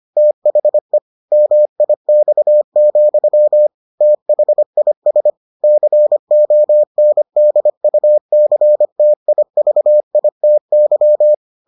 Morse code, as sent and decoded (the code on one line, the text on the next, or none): - .... / -- .. -..- --..-- / - .... .. ... / -.-. --- -. -.. ..- -.-. - .. ...- .. - -.--
THE MIX, THIS CONDUCTIVITY